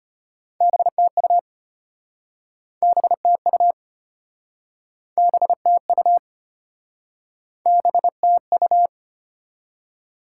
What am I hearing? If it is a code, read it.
Morse code, heard as a keyed tone, about 33 wpm, 700 Hz. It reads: BTU BTU BTU BTU